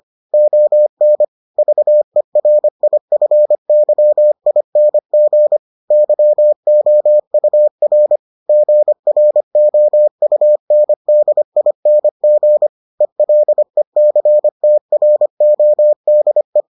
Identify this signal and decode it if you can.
Morse code, heard as a keyed tone, about 25 wpm, 600 Hz. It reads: ON VERIFYING YOUR GROUNDING ELECTRODE